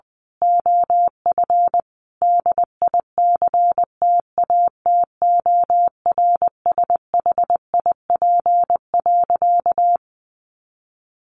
Morse code, heard as a keyed tone, about 20 wpm, 700 Hz. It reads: OF DICTATORSHIP.
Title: A Sound Of Thunder 20wpm